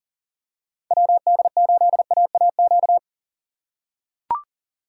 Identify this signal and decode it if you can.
Morse code, heard as a keyed tone, about 40 wpm, 700 Hz. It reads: WD8AAQ
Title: USA Call Signs - 2x Character Spacing - 40wpm